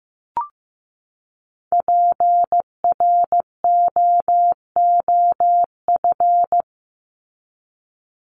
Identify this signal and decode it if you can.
Morse code, heard as a keyed tone, about 15 wpm, 700 Hz. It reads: PROOF
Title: Top 1501-1750 Words 15wpm